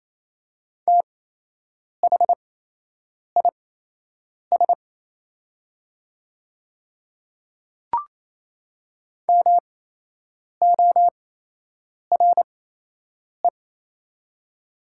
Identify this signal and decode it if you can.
Morse code, heard as a keyed tone, about 28 wpm, spaced as for 8 wpm, 700 Hz. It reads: THIS MORE
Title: 8x Character Spacing - Top 100 Words - 28wpm